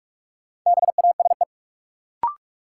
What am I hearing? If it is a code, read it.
Morse code, heard as a keyed tone, about 46 wpm, 700 Hz. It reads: BASE
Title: Sentences from Top 1000 Words - Review Part 1 - New Word Drill - 45wpm